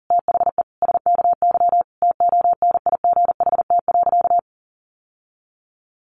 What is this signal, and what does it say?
Morse code, heard as a keyed tone, about 40 wpm, 700 Hz. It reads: THE SKY TONIGHT.
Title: Sentences from Top 1300 Words 40wpm